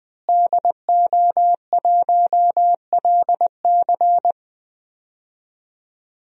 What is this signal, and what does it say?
Morse code, heard as a keyed tone, about 20 wpm, 700 Hz. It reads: DO1LC
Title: DXCC-100 Call Signs - 5 Characters - Repeated 2x - 20wpm